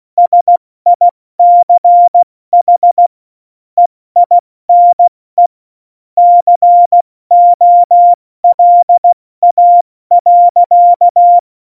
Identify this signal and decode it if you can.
Morse code, heard as a keyed tone, about 16 wpm, 700 Hz. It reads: SICH EINE COLA.
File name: Morsekurs / Witze / Witz 10.Dezember 2019, 70 F80